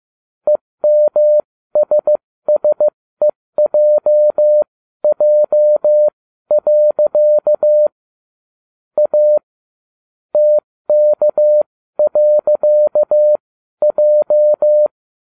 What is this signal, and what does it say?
Morse code, heard as a keyed tone, about 15 wpm, spaced as for 13 wpm, 600 Hz. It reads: EMSSEJJ. A TK.J